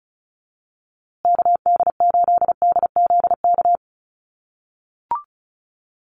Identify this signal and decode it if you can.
Morse code, heard as a keyed tone, about 34 wpm, 700 Hz. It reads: KD8DZK
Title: USA Call Signs Repeated 2x 35wpm